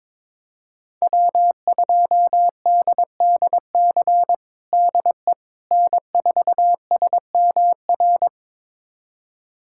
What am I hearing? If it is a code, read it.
Morse code, heard as a keyed tone, about 22 wpm, 700 Hz. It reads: W2DDC DE N4SMR
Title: USA Callsign DE Callsign - 5 Characters - 22wpm